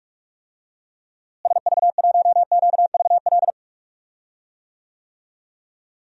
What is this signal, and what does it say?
Morse code, heard as a keyed tone, about 47 wpm, 700 Hz. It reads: SV1QVL
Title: DXCC-100 Call Signs - 6 Characters - Repeated 2x - 45wpm